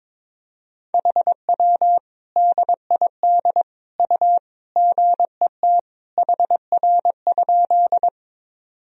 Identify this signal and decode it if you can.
Morse code, heard as a keyed tone, about 22 wpm, 700 Hz. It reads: HW DID U GET HR?
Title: Sentences from Top 200 Words and Abbreviations 22wpm